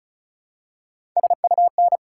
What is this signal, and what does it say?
Morse code, heard as a keyed tone, about 34 wpm, 700 Hz. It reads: SUN
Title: Three Letter Words - 35wpm